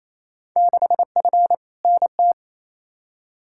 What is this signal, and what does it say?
Morse code, heard as a keyed tone, about 28 wpm, 700 Hz. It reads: 6F NT
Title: Field Day Exchanges Encore - 28wpm